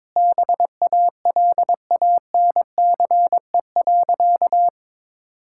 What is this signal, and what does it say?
Morse code, heard as a keyed tone, about 21 wpm, 700 Hz. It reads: BALANCE.
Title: Sentences from Top 1000 Words - Review Part 6 - 22wpm